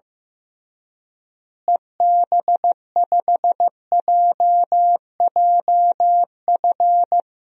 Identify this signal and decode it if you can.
Morse code, heard as a keyed tone, about 16 wpm, 700 Hz. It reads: EB5JJF
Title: DXCC-100 Call Signs - 6 Characters - Repeated 2x - 15wpm